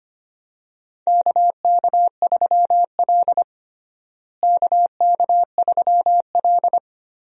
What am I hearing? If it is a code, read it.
Morse code, heard as a keyed tone, about 25 wpm, 700 Hz. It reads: KK3L KK3L
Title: USA Call Signs - 4 Characters - Repeated 2x - 25wpm